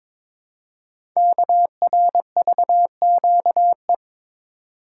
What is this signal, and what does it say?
Morse code, heard as a keyed tone, about 22 wpm, 700 Hz. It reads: KRVQE
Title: Code Groups - 5 Letters - 22wpm